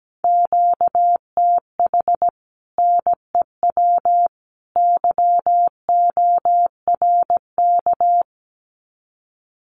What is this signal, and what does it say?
Morse code, heard as a keyed tone, about 17 wpm, 700 Hz. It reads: QTH NEW YORK
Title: Common QSO Phrases 17wpm